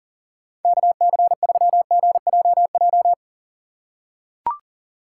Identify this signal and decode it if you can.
Morse code, heard as a keyed tone, about 40 wpm, 700 Hz. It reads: KC3GJJ
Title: USA Call Signs - 6 Characters - 6x Character Spacing - 40wpm